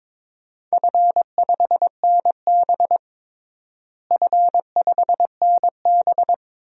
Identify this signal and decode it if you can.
Morse code, heard as a keyed tone, about 22 wpm, 700 Hz. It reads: F5NB F5NB
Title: DXCC-100 Call Signs - 4 Characters - Repeated 2x - 22wpm